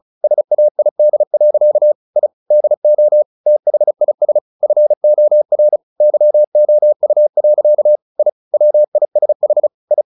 Morse code, heard as a keyed tone, about 34 wpm, 600 Hz. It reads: SAID. I DO THIS FOR YOU. I WISH I
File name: the old man and the sea / the old man and the sea 1 35wpm